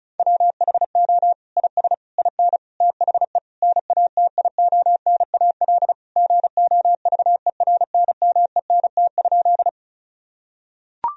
Morse code, heard as a keyed tone, about 36 wpm, 700 Hz. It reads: WHO IS IN THE NATIONAL GOVERNMENT?
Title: Sentences from Top 1000 Words - Review Part 4 - 35wpm